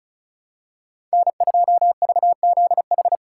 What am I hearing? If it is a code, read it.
Morse code, heard as a keyed tone, about 36 wpm, 700 Hz. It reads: N2VZH
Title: USA Call Signs - 5 Characters - 3x Character Spacing - 35wpm